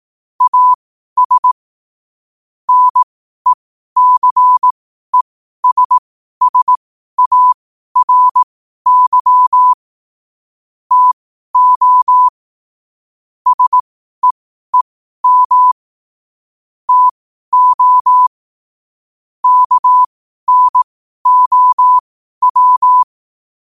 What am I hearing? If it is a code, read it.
Morse code, heard as a keyed tone, about 18 wpm, spaced as for 12 wpm, 1000 Hz. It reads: AS NECESSARY TO SEEM TO KNOW
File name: qotdfiles / 12wpm